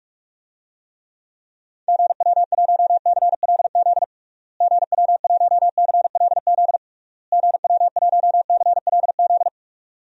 Morse code, heard as a keyed tone, about 46 wpm, 700 Hz. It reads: GW1CLZ GW1CLZ GW1CLZ
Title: DXCC-100 Call Signs - 6 Characters - Repeated 3x - 45wpm